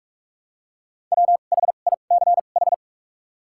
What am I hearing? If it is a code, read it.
Morse code, heard as a keyed tone, about 46 wpm, spaced as for 33 wpm, 700 Hz. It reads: WHICH
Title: 2x Character Spacing - Top 300 Words - 45wpm